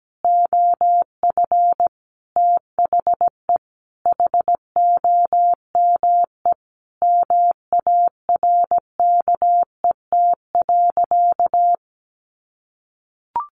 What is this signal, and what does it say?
Morse code, heard as a keyed tone, about 17 wpm, 700 Hz. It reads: OF THE HOME MARKET.
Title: Sentences from Top 1400 Words 17wpm